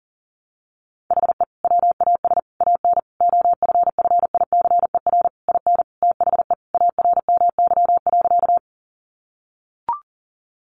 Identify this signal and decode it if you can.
Morse code, heard as a keyed tone, about 40 wpm, 700 Hz. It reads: HE WAS AN OFFICER IN THE ARMY.
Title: Sentences from Top 1200 Words 40wpm